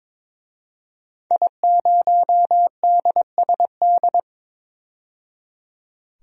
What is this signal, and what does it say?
Morse code, heard as a keyed tone, about 22 wpm, 700 Hz. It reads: I0DSD
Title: DXCC-100 Call Signs - 5 Characters - 22wpm